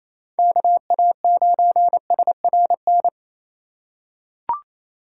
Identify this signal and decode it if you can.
Morse code, heard as a keyed tone, about 27 wpm, 700 Hz. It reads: KA9SRN
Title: USA Call Signs - 6 Characters - 3x Character Spacing - 28wpm